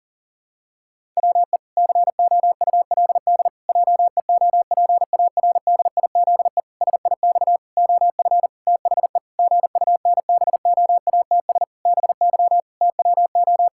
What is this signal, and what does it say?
Morse code, heard as a keyed tone, about 40 wpm, 700 Hz. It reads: WE COULD JEOPARDIZE SIX OF THE GUNBOATS BY TWO
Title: Pangram Challenge v2 - 40wpm